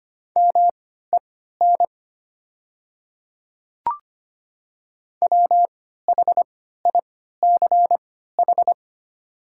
Morse code, heard as a keyed tone, about 25 wpm, spaced as for 14 wpm, 700 Hz. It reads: MEN WHICH
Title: 3x Character Spacing - Top 200 Words - 25wpm